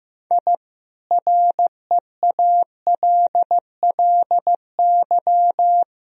Morse code, heard as a keyed tone, about 15 wpm, 700 Hz. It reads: I REALLY
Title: Sentences from Top 1000 Words - Review Part 4 - 15wpm